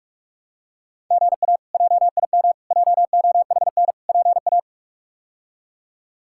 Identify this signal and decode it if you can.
Morse code, heard as a keyed tone, about 46 wpm, 700 Hz. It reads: GA JIM JOHN PA
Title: K1USN SST QSO Practice - 45wpm